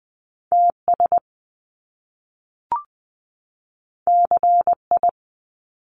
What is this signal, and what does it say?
Morse code, heard as a keyed tone, about 19 wpm, 700 Hz. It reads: TS CI
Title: Top Two Letter Combinations in Words 20wpm